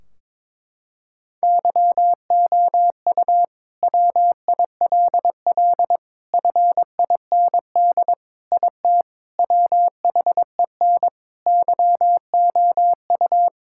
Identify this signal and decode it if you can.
Morse code, heard as a keyed tone, about 22 wpm, 700 Hz. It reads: YOU WILL FIND IT WHEN YOU
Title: Sentences from Top 1600 Words 22wpm